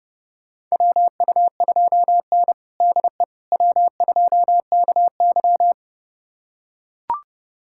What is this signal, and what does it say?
Morse code, heard as a keyed tone, about 30 wpm, 700 Hz. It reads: WU2N DE W2KY
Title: USA Callsign DE Callsign - 4 Characters - 30wpm